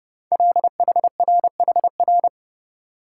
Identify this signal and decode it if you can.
Morse code, heard as a keyed tone, about 30 wpm, 700 Hz. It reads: LHRHR
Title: Code Groups - 5 Letter:Numbers - 30wpm